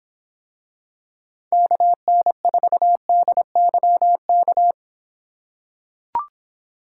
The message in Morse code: KN4DYK

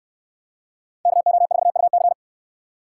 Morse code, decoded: KO4UC

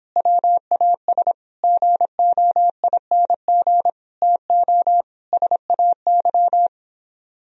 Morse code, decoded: WAS GOING TO SAY

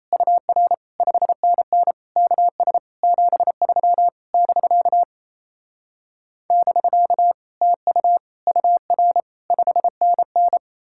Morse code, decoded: UR 5NN KS 73 <BK> <BK> TU UR 5NN

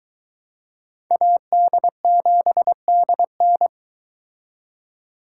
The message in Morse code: AD7DN

